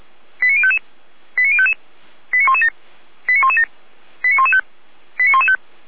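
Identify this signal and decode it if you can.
Five-tone selective call (ZVEI), heard as quick runs of five tones, tones of about 70 ms. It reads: 89055 89055 89117 89117 89115 89115
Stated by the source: ZVEI five-tone selective call received on an SDR.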